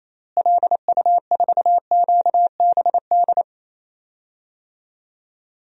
Morse code, heard as a keyed tone, about 28 wpm, 700 Hz. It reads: LU4QBD
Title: DXCC-100 Call Signs - 6 Characters - Repeated 3x - 28wpm